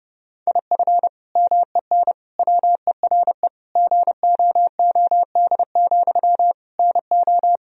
Morse code, heard as a keyed tone, about 30 wpm, 700 Hz. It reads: IF MEN WERE GOOD, NO
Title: Sentences from Top 1000 Words - Review Part 3 - 30wpm